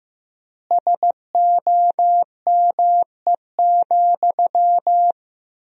Morse code, heard as a keyed tone, about 15 wpm, 700 Hz. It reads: SOME,
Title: 3x Word-Spacing - Sentences from Top 200 Words - 15wpm